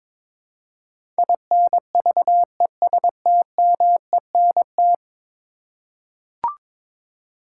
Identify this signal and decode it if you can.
Morse code, heard as a keyed tone, about 22 wpm, 700 Hz. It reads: INVESTMENT
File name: practice / 22wpm / 10 Letter Words - First 3 Letters Given - 22wpm